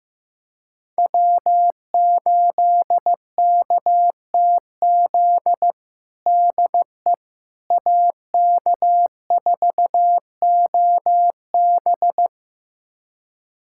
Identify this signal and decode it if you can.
Morse code, heard as a keyed tone, about 16 wpm, 700 Hz. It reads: W8KTZ DE AK4OB
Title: USA Callsign DE Callsign - 5 Characters - 15wpm